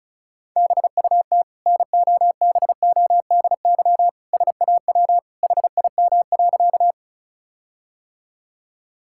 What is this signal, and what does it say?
Morse code, heard as a keyed tone, about 36 wpm, 700 Hz. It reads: BUT NOBODY SAW HIM.
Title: Sentences from Top 1600 Words 35wpm